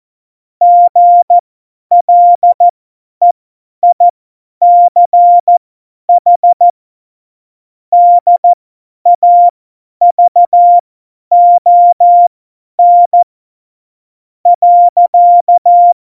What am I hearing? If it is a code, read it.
Morse code, heard as a keyed tone, about 14 wpm, spaced as for 9 wpm, 700 Hz. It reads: GLEICH DAVON.
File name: Morsekurs / Witze / Witz 06 March 2018, 50 F70